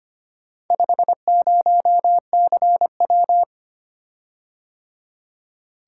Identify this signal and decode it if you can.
Morse code, heard as a keyed tone, about 25 wpm, 700 Hz. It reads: 50CW
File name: practice / 25wpm / Four Letter-Number Combinations - 25wpm